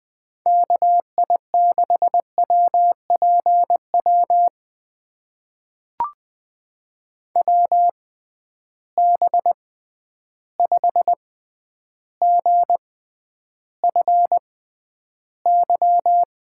Morse code, heard as a keyed tone, about 20 wpm, spaced as for 10 wpm, 700 Hz. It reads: KI6WPW WB5GFY
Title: USA Call Signs - 6 Characters - 6x Character Spacing - 20wpm